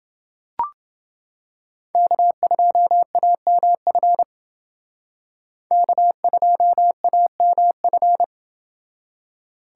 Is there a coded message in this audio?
Yes: Morse code from 2.0 to 8.2 s, heard as a keyed tone, about 28 wpm, 700 Hz. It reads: K2AMF K2AMF